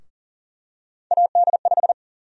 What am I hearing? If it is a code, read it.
Morse code, heard as a keyed tone, about 40 wpm, 700 Hz. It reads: AD5
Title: USA Call Sign Prefixes - 40wpm